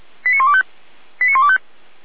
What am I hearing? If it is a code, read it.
Five-tone selective call (ZVEI), heard as quick runs of five tones, tones of about 70 ms. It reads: 89125 89125